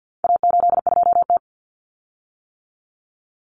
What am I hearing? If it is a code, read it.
Morse code, heard as a keyed tone, about 50 wpm, 700 Hz. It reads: A82T
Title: Common Cut Numbers - 4-Digit 50wpm